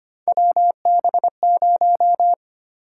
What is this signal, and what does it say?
Morse code, heard as a keyed tone, about 25 wpm, 700 Hz. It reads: WB0
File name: Three Letter-Number Combinations - 25wpm